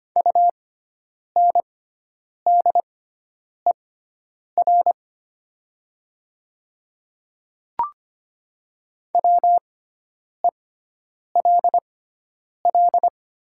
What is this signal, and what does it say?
Morse code, heard as a keyed tone, about 25 wpm, spaced as for 9 wpm, 700 Hz. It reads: UNDER WELL
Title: 6x Character Spacing - Top 200 Words - 25wpm